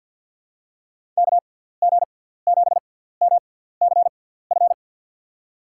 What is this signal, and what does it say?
Morse code, heard as a keyed tone, about 47 wpm, spaced as for 17 wpm, 700 Hz. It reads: KG7MCF
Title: USA Call Signs - 6x Character Spacing - 50wpm